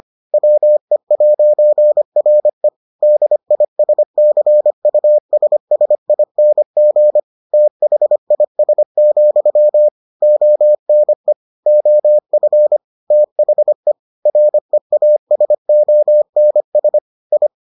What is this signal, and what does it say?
Morse code, heard as a keyed tone, about 25 wpm, 600 Hz. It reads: WE'RE DISCUSSING THIS, ONE OF THE REASONS I